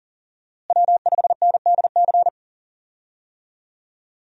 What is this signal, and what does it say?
Morse code, heard as a keyed tone, about 37 wpm, 700 Hz. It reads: W5NDC
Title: USA Call Signs - 5 Characters - Repeated 3x - 40wpm